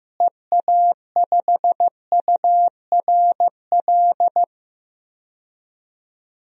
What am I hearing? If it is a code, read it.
Morse code, heard as a keyed tone, about 16 wpm, 700 Hz. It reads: EA5URL
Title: DXCC-100 Call Signs - 6 Characters - Repeated 2x - 15wpm